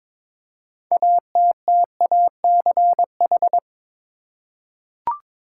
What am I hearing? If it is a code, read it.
Morse code, heard as a keyed tone, about 22 wpm, 700 Hz. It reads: ATTACH
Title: Six Letter Words - 22wpm